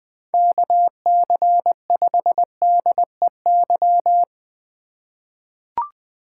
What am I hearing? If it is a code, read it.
Morse code, heard as a keyed tone, about 20 wpm, 700 Hz. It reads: KC5DEY
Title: USA Call Signs - 6 Characters - 4x Character Spacing - 20wpm